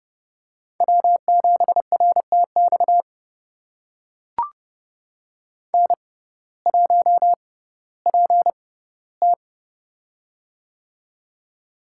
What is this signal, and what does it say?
Morse code, heard as a keyed tone, about 31 wpm, spaced as for 17 wpm, 700 Hz. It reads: W7RTX N1PT
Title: USA Call Signs - 6x Character Spacing - 30wpm